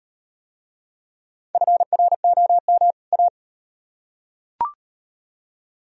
Morse code, heard as a keyed tone, about 36 wpm, 700 Hz. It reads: FROM A